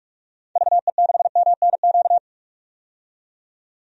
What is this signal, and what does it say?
Morse code, heard as a keyed tone, about 46 wpm, 700 Hz. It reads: VE6MNQ
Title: DXCC-100 Call Signs - 6 Characters - 45wpm